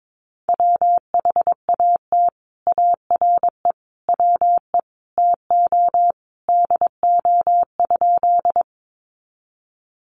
Morse code, heard as a keyed tone, about 22 wpm, 700 Hz. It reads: WHAT ARE WE TO DO?
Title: Sentences from Top 100 Words 22wpm